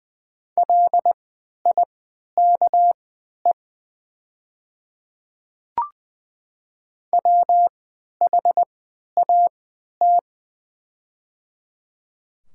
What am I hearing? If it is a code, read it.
Morse code, heard as a keyed tone, about 20 wpm, spaced as for 11 wpm, 700 Hz. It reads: LIKE WHAT
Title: 3x Character Spacing - Top 200 Words - 20wpm